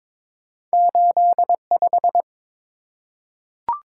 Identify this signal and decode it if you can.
Morse code, heard as a keyed tone, about 22 wpm, 700 Hz. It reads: 85
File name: practice / 22wpm / Two Letter-Number Combinations - 22wpm